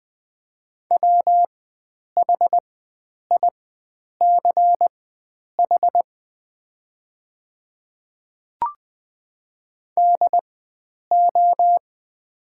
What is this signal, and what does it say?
Morse code, heard as a keyed tone, about 20 wpm, spaced as for 9 wpm, 690 Hz. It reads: WHICH DO
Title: 4x Character Spacing - Top 100 Words - 20wpm